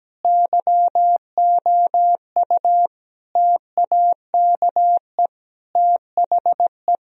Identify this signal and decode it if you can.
Morse code, heard as a keyed tone, about 17 wpm, 700 Hz. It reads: YOU TAKE THE
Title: Sentences from Top 1000 Words - Review Part 3 - 17wpm